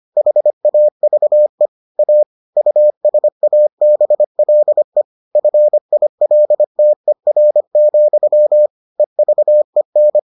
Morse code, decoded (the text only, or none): HAVE A USABLE FILTER, EVEN